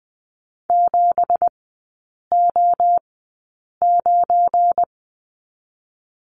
Morse code, heard as a keyed tone, about 20 wpm, spaced as for 8 wpm, 690 Hz. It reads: 7O9
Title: 212 Lesson 34e - Number 8 - Three Characters 20wpm